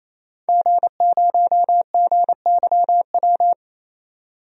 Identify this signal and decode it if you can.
Morse code, heard as a keyed tone, about 28 wpm, 700 Hz. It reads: G0GYW
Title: DXCC-100 Call Signs - 5 Characters - 28wpm